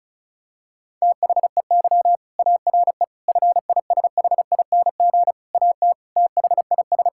Morse code, heard as a keyed tone, about 36 wpm, 700 Hz. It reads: THEY ARE FISHING AT THIS
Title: Sentences from Top 1000 Words - Review Part 5 - 35wpm